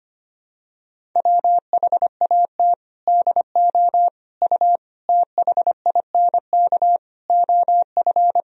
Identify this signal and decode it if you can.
Morse code, heard as a keyed tone, about 25 wpm, 700 Hz. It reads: WHAT DO U THINK OF
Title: Sentences from Top 200 Words and Abbreviations 25wpm